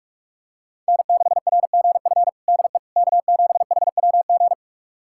Morse code, heard as a keyed tone, about 46 wpm, 700 Hz. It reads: N6RGF DE K7HWG
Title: USA Callsign DE Callsign - 5 Characters - 45wpm